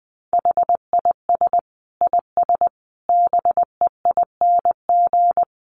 Morse code, heard as a keyed tone, about 20 wpm, 700 Hz. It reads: HIS IS BEING
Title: Sentences from Top 600 Words 20wpm